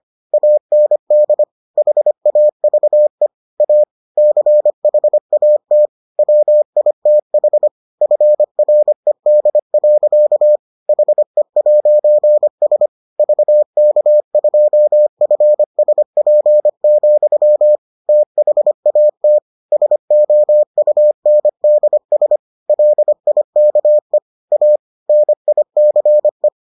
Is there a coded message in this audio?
Yes: Morse code from 0.3 to 26.5 s, heard as a keyed tone, about 25 wpm, 600 Hz. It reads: AND HAVE A CHAT WITH FRED. HE'S VK2FSP, THAT SOUNDS LIKE A NICE